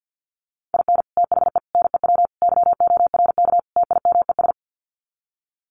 Morse code, heard as a keyed tone, about 48 wpm, 700 Hz. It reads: IN THE NEW YORK TIMES
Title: Sets of 5 Words 50wpm